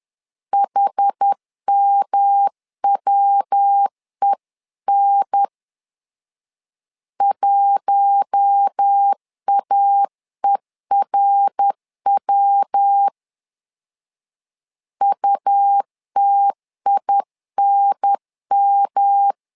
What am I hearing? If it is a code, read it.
Morse code, heard as a keyed tone, about 10 wpm, 800 Hz. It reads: HMWEN 1AERW UTINM
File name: LEZIONI / 4 Q1 10wpm 2448